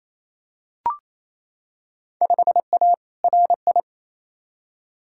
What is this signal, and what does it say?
Morse code, heard as a keyed tone, about 28 wpm, 700 Hz. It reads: 5A RI
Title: Field Day Exchanges Encore - 28wpm